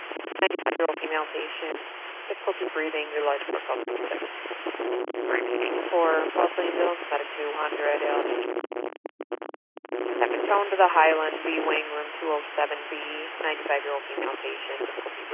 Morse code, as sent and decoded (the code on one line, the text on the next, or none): none